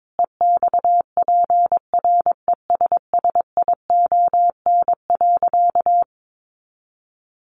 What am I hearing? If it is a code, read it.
Morse code, heard as a keyed tone, about 22 wpm, 700 Hz. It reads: EXPRESSION.
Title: Sentences from Top 1200 Words 22wpm